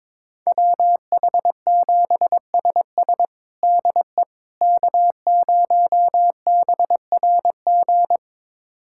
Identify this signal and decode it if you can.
Morse code, heard as a keyed tone, about 22 wpm, 700 Hz. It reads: WH7SS DE K0BRG
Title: USA Callsign DE Callsign - 5 Characters - 22wpm